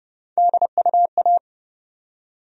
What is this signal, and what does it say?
Morse code, heard as a keyed tone, about 32 wpm, 700 Hz. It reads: DUA
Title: USA Call Sign Postfixes - 3 Letters 30wpm